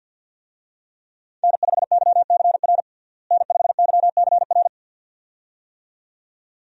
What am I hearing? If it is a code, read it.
Morse code, heard as a keyed tone, about 52 wpm, 700 Hz. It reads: N5YCR N5YCR
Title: USA Call Signs - 5 Characters - Repeated 2x - 50wpm